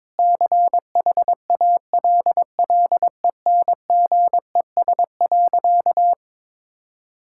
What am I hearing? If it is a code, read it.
Morse code, heard as a keyed tone, about 22 wpm, 700 Hz. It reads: CHALLENGES.
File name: practice / 22wpm / Sentences with 7-Word Phrases - 22wpm